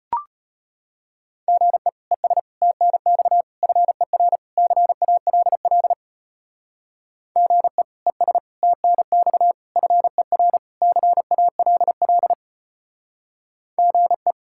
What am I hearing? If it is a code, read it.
Morse code, heard as a keyed tone, about 31 wpm, 700 Hz. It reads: GE ES TNX FER CALL GE ES TNX FER CALL GE
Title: Speed-Racing Common QSO Phrases Encore - 25wpm